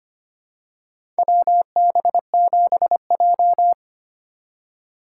Morse code, heard as a keyed tone, about 25 wpm, 700 Hz. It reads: WB7J
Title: USA Call Signs - 4 Characters - 4x Character Spacing - 25wpm